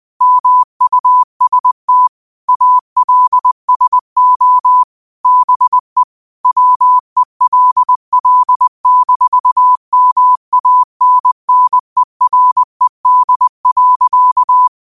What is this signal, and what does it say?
Morse code, heard as a keyed tone, about 20 wpm, 1000 Hz. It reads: MUST ALSO BE WELL-MANNERED.